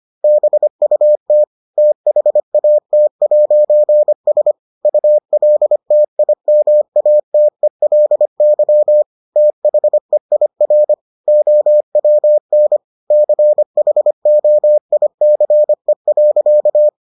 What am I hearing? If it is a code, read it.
Morse code, heard as a keyed tone, about 25 wpm, 600 Hz. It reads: BUT THAT'S ULTIMATELY THEIR OWN CHOICE.